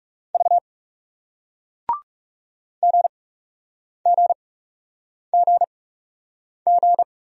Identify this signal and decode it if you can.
Morse code, heard as a keyed tone, about 39 wpm, spaced as for 8 wpm, 700 Hz. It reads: V GGGG